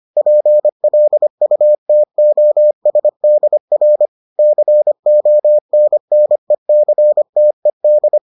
Morse code, heard as a keyed tone, about 25 wpm, 600 Hz. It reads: PLUTOSDR CONNECTED